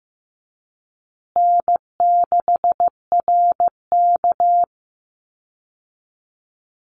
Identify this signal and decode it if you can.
Morse code, heard as a keyed tone, about 15 wpm, 700 Hz. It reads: N6RK
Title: Easy Call Signs Repeated 2x - 15wpm